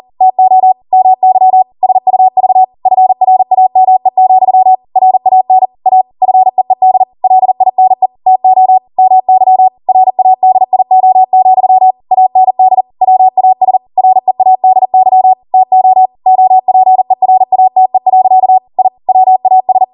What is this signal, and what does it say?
Morse code, heard as a keyed tone, about 40 wpm, 750 Hz. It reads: TO MY SUV FRAME, RAN A FEED LINE TO MY RADIO, AND WAS READY TO OPERATE. I WAS